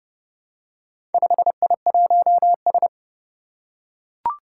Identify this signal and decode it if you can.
Morse code, heard as a keyed tone, about 30 wpm, 700 Hz. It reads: 5I1S
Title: Four Challenging Letter-Number Combinations 30wpm